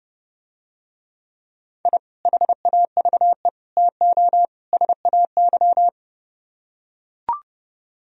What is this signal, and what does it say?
Morse code, heard as a keyed tone, about 30 wpm, 700 Hz. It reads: I HAVE TO SAY